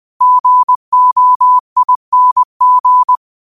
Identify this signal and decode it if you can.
Morse code, heard as a keyed tone, about 20 wpm, 1000 Hz. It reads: GOING
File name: qotdfiles / 20wpm